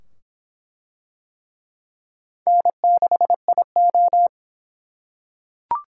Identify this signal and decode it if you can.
Morse code, heard as a keyed tone, about 25 wpm, 700 Hz. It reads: N6IO